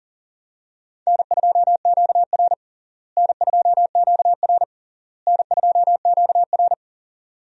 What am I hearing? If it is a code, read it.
Morse code, heard as a keyed tone, about 40 wpm, 700 Hz. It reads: N2QR N2QR N2QR
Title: USA Call Signs - 4 Characters - Repeated 3x - 40wpm